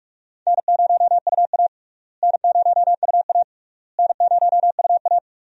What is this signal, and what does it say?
Morse code, heard as a keyed tone, about 46 wpm, 700 Hz. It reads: N0UA N0UA N0UA
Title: USA Call Signs - 4 Characters - Repeated 3x - 45wpm